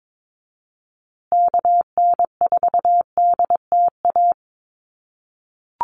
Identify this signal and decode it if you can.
Morse code, heard as a keyed tone, about 22 wpm, 700 Hz. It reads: KN4DTA
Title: USA Call Signs Repeated 2x 22wpm